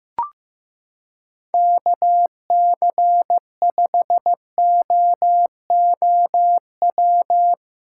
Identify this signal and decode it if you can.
Morse code, heard as a keyed tone, about 15 wpm, 700 Hz. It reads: KC5OOW
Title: USA Callsign DE Callsign - 6 Characters - 15wpm